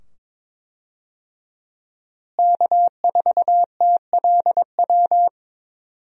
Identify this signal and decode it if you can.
Morse code, heard as a keyed tone, about 21 wpm, 700 Hz. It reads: K4TLW
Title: USA Call Signs - 5 Characters - Repeated 2x - 22wpm